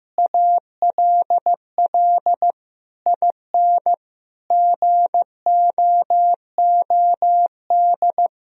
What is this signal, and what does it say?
Morse code, heard as a keyed tone, about 15 wpm, 700 Hz. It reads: ALL IN GOOD
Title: Sentences from Top 1000 Words - Review Part 3 - 15wpm